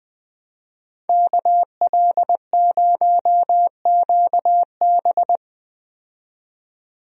Morse code, heard as a keyed tone, about 20 wpm, 700 Hz. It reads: KL0QB
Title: USA Call Signs - 5 Characters - 20wpm